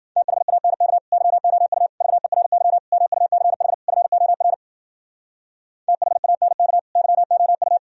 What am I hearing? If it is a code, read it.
Morse code, heard as a keyed tone, about 51 wpm, 700 Hz. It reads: THANK YOU VERY MUCH FOR THANK YOU